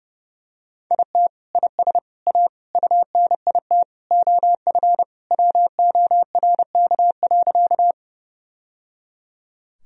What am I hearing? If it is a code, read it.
Morse code, heard as a keyed tone, about 30 wpm, 700 Hz. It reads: IT IS A UNIT OF WORK.
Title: Sentences from Top 1000 Words - Review Part 2 - 30wpm